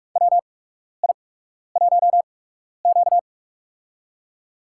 Morse code, heard as a keyed tone, about 45 wpm, spaced as for 12 wpm, 700 Hz. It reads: WI1Q
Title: USA Call Signs - 4 Characters - 8x Character Spacing - 45wpm